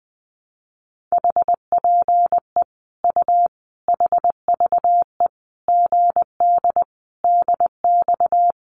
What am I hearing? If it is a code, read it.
Morse code, heard as a keyed tone, about 20 wpm, 700 Hz. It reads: HPE U HVE GD DX
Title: Common QSO Phrases 20wpm